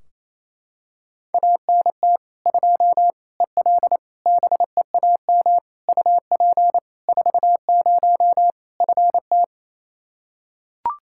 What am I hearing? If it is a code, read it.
Morse code, heard as a keyed tone, about 28 wpm, 700 Hz. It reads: ANT 2 EL BEAM UP 40 FT
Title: K1USN SST QSO Practice 28wpm